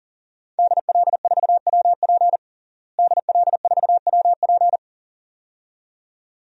Morse code, heard as a keyed tone, about 40 wpm, 700 Hz. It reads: DL4WP DL4WP
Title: DXCC-100 Call Signs - 5 Characters - Repeated 2x - 40wpm